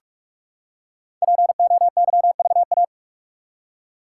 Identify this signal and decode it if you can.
Morse code, heard as a keyed tone, about 44 wpm, 700 Hz. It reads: POYVA
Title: Code Groups - 5 Letters - 45wpm